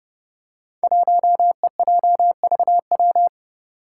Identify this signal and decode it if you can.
Morse code, heard as a keyed tone, about 30 wpm, 700 Hz. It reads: 1EJVW
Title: Five Challenging Letter-Number Combinations 30wpm